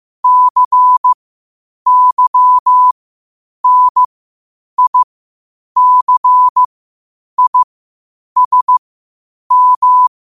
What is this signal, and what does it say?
Morse code, heard as a keyed tone, about 15 wpm, spaced as for 8 wpm, 1000 Hz. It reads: CYNICISM
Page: qotd-at-8-wpm